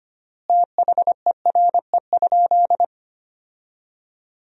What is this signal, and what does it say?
Morse code, heard as a keyed tone, about 25 wpm, 700 Hz. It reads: THERE?
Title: 4x Word-Spacing - Sentences from Top 100 Words - 25wpm